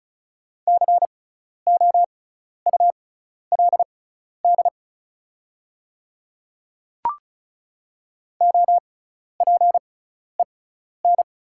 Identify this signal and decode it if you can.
Morse code, heard as a keyed tone, about 36 wpm, spaced as for 12 wpm, 690 Hz. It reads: COULD OPEN
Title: 6x Character Spacing - Top 500 Words - 35wpm